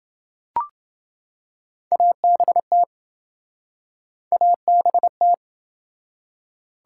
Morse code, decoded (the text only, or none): ABT ABT